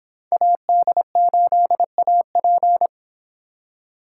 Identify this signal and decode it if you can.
Morse code, heard as a keyed tone, about 26 wpm, 700 Hz. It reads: AD8AP